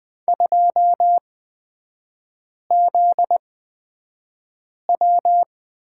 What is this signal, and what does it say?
Morse code, heard as a keyed tone, about 20 wpm, spaced as for 5 wpm, 700 Hz. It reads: 2ZW
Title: 243 Lesson 39e - Letter z - Three Characters 20wpm